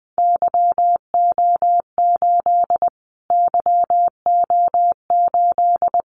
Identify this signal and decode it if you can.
Morse code, heard as a keyed tone, about 19 wpm, 700 Hz. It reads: YO8 YO8
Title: Callsign Prefixes for Top 25 DXCC - Repeated 3x - 20wpm